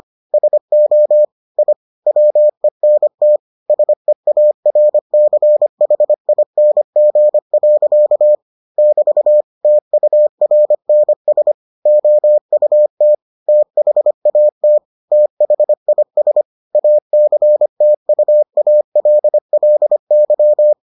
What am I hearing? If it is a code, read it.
Morse code, heard as a keyed tone, about 25 wpm, 600 Hz. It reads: SO I WENT SEARCHING. <BT> TURNS OUT THAT THIS ACTUALLY